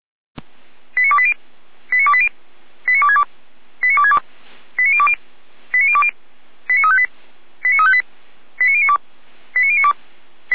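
Five-tone selective call (ZVEI), heard as quick runs of five tones, tones of about 70 ms. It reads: 89280 89280 89262 89262 89020 89020 89358 89358 89092 89092
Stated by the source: ZVEI five-tone selective call received by SDR.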